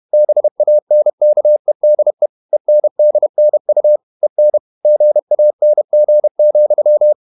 Morse code, heard as a keyed tone, about 31 wpm, 600 Hz. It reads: BANKEDE ENDNU EN GANG,
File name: Ferie QRQ 31wpm